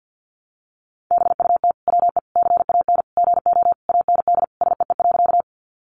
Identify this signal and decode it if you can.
Morse code, heard as a keyed tone, about 48 wpm, 700 Hz. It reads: BUT WE CAN GO AND SEE.
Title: Sentences from Top 100 Words 50wpm